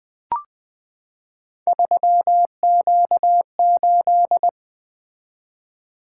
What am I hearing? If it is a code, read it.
Morse code, heard as a keyed tone, about 20 wpm, 700 Hz. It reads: 3Q8
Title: Three Letter-Number Combinations - 20wpm